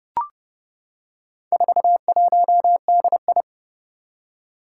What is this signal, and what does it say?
Morse code, heard as a keyed tone, about 30 wpm, 700 Hz. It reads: 41DI
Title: Four Challenging Letter-Number Combinations 30wpm